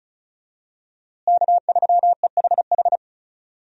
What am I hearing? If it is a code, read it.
Morse code, heard as a keyed tone, about 36 wpm, 700 Hz. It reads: K3EHH
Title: USA Call Signs - 5 Characters - Repeated 3x - 35wpm